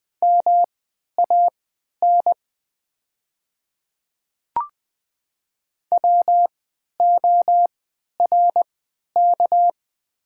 Morse code, heard as a keyed tone, about 20 wpm, spaced as for 11 wpm, 700 Hz. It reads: MAN WORK
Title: 3x Character Spacing - Top 200 Words - 20wpm